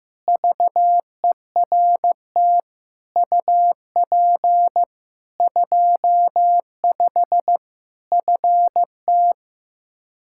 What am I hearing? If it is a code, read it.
Morse code, heard as a keyed tone, about 15 wpm, 700 Hz. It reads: VERT UP 25 FT
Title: Practice QSO - Exchange 2 - Info - 15wpm